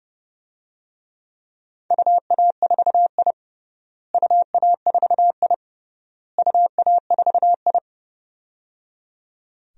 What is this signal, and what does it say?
Morse code, heard as a keyed tone, about 30 wpm, 700 Hz. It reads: UA4I UA4I UA4I
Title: DXCC-100 Call Signs - 4 Characters - Repeated 3x - 30wpm